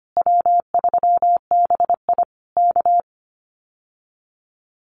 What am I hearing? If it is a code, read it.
Morse code, heard as a keyed tone, about 25 wpm, 700 Hz. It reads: W3BI K
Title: Common QSO Phrases Encore 25wpm